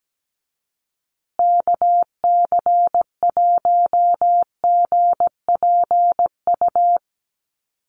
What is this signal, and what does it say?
Morse code, heard as a keyed tone, about 17 wpm, 700 Hz. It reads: KC1GPU
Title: USA Call Signs Repeated 2x 17wpm